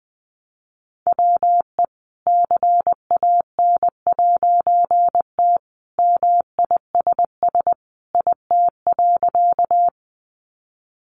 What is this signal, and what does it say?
Morse code, heard as a keyed tone, about 20 wpm, 700 Hz. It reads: WE CAN'T MISS IT.
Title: A Sound Of Thunder 20wpm